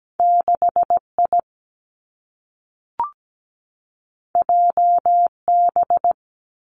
Morse code, heard as a keyed tone, about 17 wpm, 700 Hz. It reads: 6 I J B
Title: Two Challenging Letter-Number Combinations 17wpm